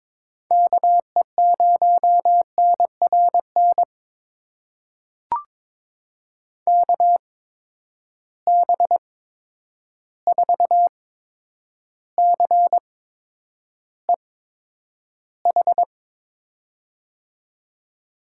Morse code, heard as a keyed tone, about 22 wpm, spaced as for 9 wpm, 700 Hz. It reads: KE0NRN KB4CEH
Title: USA Call Signs - 6 Characters - 8x Character Spacing - 22wpm